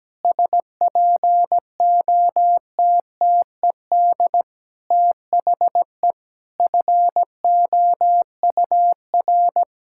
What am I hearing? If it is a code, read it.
Morse code, heard as a keyed tone, about 17 wpm, 700 Hz. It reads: SPOTTED THE FOUR
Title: Pangram Challenge v2 - 17wpm